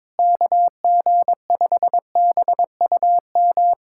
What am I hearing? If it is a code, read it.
Morse code, heard as a keyed tone, about 22 wpm, 700 Hz. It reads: KG5BUM
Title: USA Callsign DE Callsign - 6 Characters - 22wpm